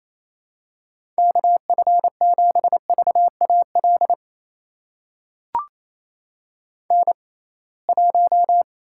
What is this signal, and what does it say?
Morse code, heard as a keyed tone, about 28 wpm, spaced as for 21 wpm, 700 Hz. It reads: KF7VAL N1
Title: USA Call Signs - 6x Character Spacing - 28wpm